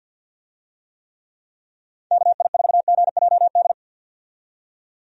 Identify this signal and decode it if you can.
Morse code, heard as a keyed tone, about 49 wpm, 700 Hz. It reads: KI4GJD
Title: USA Call Signs - 6 Characters - 50wpm